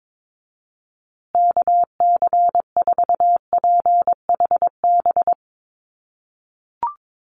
Morse code, heard as a keyed tone, about 22 wpm, 700 Hz. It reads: KC4PHB
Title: USA Call Signs Repeated 2x 22wpm